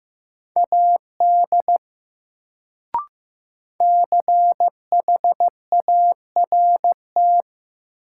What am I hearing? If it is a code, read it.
Morse code, heard as a keyed tone, about 15 wpm, 700 Hz. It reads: AD CHART
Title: Sentences from Top 1300 Words - New Word Drill - 15wpm